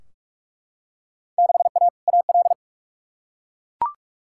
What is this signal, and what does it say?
Morse code, heard as a keyed tone, about 42 wpm, 700 Hz. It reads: 6A AL
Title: Field Day Exchanges Encore - 45wpm